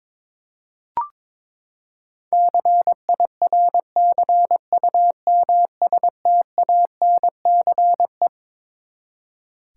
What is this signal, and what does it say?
Morse code, heard as a keyed tone, about 22 wpm, 700 Hz. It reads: CIRCUMSTANCE
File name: Twelve Letter Words - 22wpm